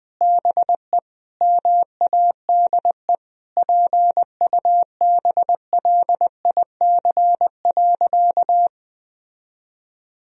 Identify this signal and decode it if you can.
Morse code, heard as a keyed tone, about 20 wpm, 700 Hz. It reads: BE MADE PUBLIC.
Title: Sentences from Top 1000 Words - Review Part 3 - 20wpm